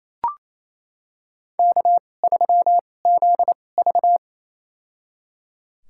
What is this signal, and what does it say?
Morse code, heard as a keyed tone, about 28 wpm, spaced as for 20 wpm, 700 Hz. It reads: K3ZV
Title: USA Call Signs - 4 Characters - 2x Character Spacing - 28wpm